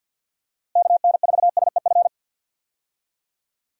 Morse code, heard as a keyed tone, about 52 wpm, 700 Hz. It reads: KN4SEF
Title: USA Call Signs - 6 Characters - Repeated 2x - 50wpm